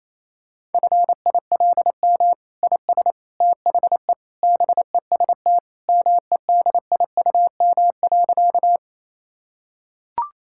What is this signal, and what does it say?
Morse code, heard as a keyed tone, about 28 wpm, 700 Hz. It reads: FILM IS THE BEST MEDIUM.
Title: Sentences from Top 1000 Words - Review Part 6 - 28wpm